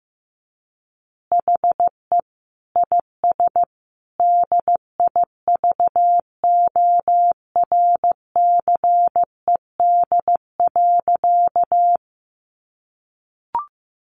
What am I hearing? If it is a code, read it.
Morse code, heard as a keyed tone, about 15 wpm, 700 Hz. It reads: HE IS DIVORCED.
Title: Additional Easy Phrases 15wpm